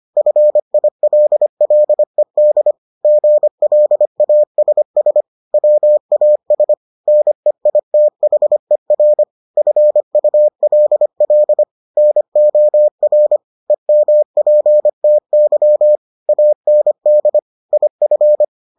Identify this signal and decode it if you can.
Morse code, heard as a keyed tone, about 25 wpm, 600 Hz. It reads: FILLED GLASS WAS NEITHER FULL NOR EMPTY AND IF